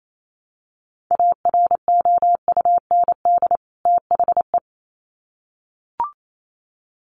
Morse code, heard as a keyed tone, about 28 wpm, 700 Hz. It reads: AROUND THE
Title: Sets of 2 Words 28wpm